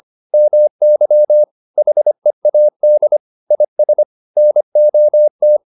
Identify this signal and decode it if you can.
Morse code, heard as a keyed tone, about 25 wpm, 600 Hz. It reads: MY HEAD IS NOT